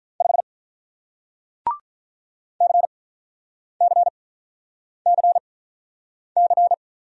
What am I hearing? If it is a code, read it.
Morse code, heard as a keyed tone, about 46 wpm, spaced as for 9 wpm, 700 Hz. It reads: 5 CCCC